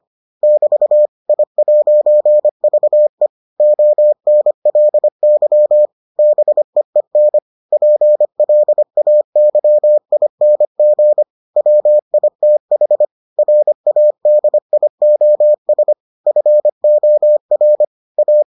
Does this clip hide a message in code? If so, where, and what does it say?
Yes: Morse code from 0.4 to 18.4 s, heard as a keyed tone, about 25 wpm, 600 Hz. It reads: <BT> I'VE ONLY BEEN PLAYING WITH RADIOS FOR A